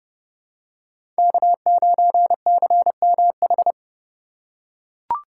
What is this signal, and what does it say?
Morse code, heard as a keyed tone, about 30 wpm, 700 Hz. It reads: K9CMH
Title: USA Call Signs - 5 Characters - 3x Character Spacing - 30wpm